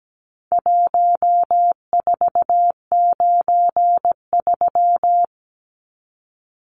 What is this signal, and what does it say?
Morse code, heard as a keyed tone, about 17 wpm, 700 Hz. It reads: 1493
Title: Numbers - Contest Style - 4-Digit 17wpm